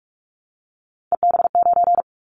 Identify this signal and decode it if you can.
Morse code, heard as a keyed tone, about 44 wpm, 700 Hz. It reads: EB9
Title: Callsign Prefixes for Top 100 DXCC - 45wpm